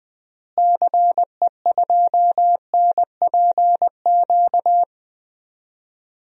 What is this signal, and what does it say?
Morse code, heard as a keyed tone, about 20 wpm, 700 Hz. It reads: CE2NPQ
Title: DXCC-100 Call Signs - 6 Characters - Repeated 3x - 20wpm